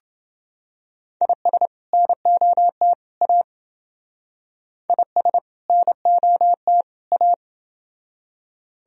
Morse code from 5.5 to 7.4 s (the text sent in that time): NOT A